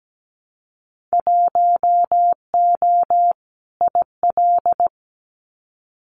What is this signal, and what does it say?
Morse code, heard as a keyed tone, about 17 wpm, 700 Hz. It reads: 1O IL
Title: Winter Field Day Exchanges 17wpm